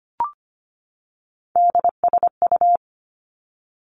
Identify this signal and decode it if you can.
Morse code, heard as a keyed tone, about 25 wpm, 700 Hz. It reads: DSU
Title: Three Challenging Letter-Number Combinations 25wpm